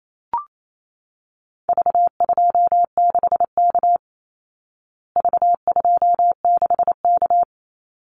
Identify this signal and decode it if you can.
Morse code, heard as a keyed tone, about 28 wpm, 700 Hz. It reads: V26K V26K
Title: Call Signs Repeated 2x - 28wpm